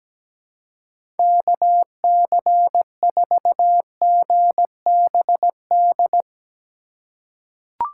Morse code, heard as a keyed tone, about 17 wpm, 700 Hz. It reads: KC4GBD
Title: USA Call Signs - 2x Character Spacing - 17wpm